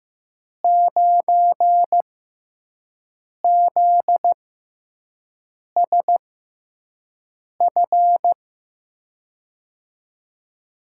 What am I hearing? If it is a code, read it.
Morse code, heard as a keyed tone, about 15 wpm, spaced as for 5 wpm, 700 Hz. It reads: 9ZSF
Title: USA Call Signs - 6x Character Spacing - 15wpm